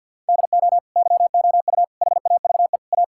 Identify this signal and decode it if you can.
Morse code, heard as a keyed tone, about 50 wpm, 700 Hz. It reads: DO YOU HAVE A
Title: Sentences from Top 1600 Words 50wpm